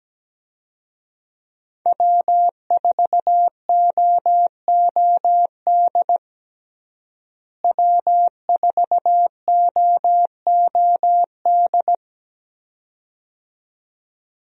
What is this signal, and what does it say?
Morse code, heard as a keyed tone, about 17 wpm, 700 Hz. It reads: W4OOD W4OOD
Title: USA Call Signs - 5 Characters - Repeated 2x - 17wpm